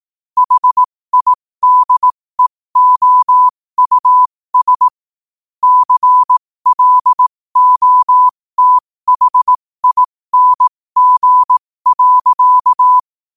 Morse code, heard as a keyed tone, about 18 wpm, spaced as for 15 wpm, 1000 Hz. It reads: HIDEOUS CLOTHING.
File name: qotdfiles / 15wpm